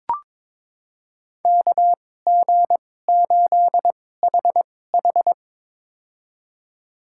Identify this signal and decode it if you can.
Morse code, heard as a keyed tone, about 22 wpm, spaced as for 16 wpm, 700 Hz. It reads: KG8HH
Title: USA Call Signs - 5 Characters - 2x Character Spacing - 22wpm